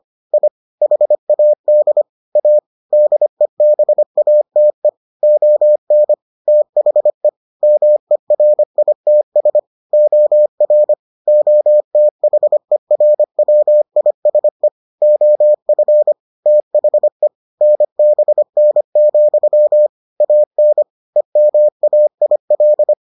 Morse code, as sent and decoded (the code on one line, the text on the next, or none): .. / .... .- -.. / .- / -.. -... .- - . / --- -. / - .... / -- . .-. .. - ... / --- .-. / --- - .... .-. .-- .. ... / --- ..-. / - .... / -. -... -. --..-- / .- -. / . -- .- .. .-..
I HAD A DEBATE ON THE MERITS OR OTHERWISE OF THE NBN, AN EMAIL